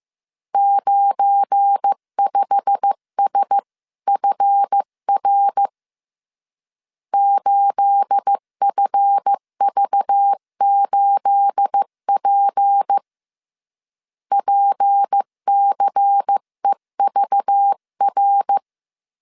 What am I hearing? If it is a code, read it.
Morse code, heard as a keyed tone, about 15 wpm, 800 Hz. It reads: 95SFR 8FV8P PCEVR
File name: CW / DET / 1 9 1500C S B 14wpm 3302